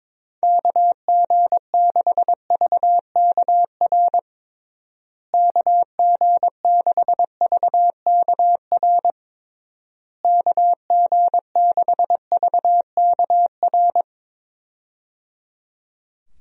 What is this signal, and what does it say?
Morse code, heard as a keyed tone, about 22 wpm, 700 Hz. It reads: KG6VKR KG6VKR KG6VKR
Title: USA Call Signs - 6 Characters - Repeated 3x - 22wpm